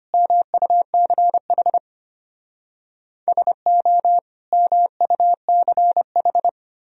Morse code, heard as a keyed tone, about 32 wpm, 700 Hz. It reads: MUCH SO MUCH